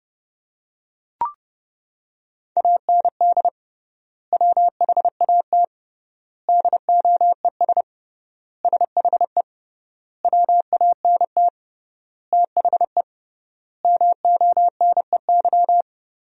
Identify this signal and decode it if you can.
Morse code, heard as a keyed tone, about 32 wpm, 700 Hz. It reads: AND WHAT DOES SHE WANT THE MONEY
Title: 3x Word-Spacing - Sentences from Top 400 Words - 30wpm